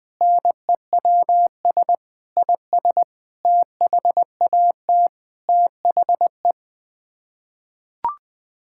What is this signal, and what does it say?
Morse code, heard as a keyed tone, about 20 wpm, 700 Hz. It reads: NEWS IS THAT THE
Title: Sets of 6 Words - 20wpm